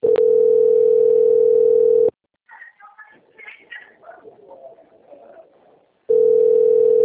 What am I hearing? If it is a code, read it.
Ringback tone (North America), 440 + 480 Hz.
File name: out- - - - - .23585